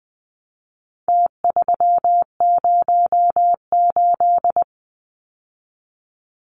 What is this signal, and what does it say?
Morse code, heard as a keyed tone, about 20 wpm, 700 Hz. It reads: T308
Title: Numbers - Contest Style - 4-Digit 20wpm